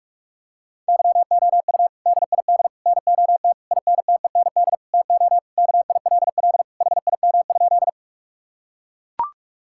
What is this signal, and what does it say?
Morse code, heard as a keyed tone, about 41 wpm, 700 Hz. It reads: YOU DID NOT INTEND TO KILL HIM?
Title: Sentences from Top 1600 Words 45wpm